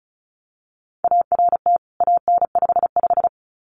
Morse code, heard as a keyed tone, about 34 wpm, 700 Hz. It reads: ART AN55
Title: CWT Exchange 35wpm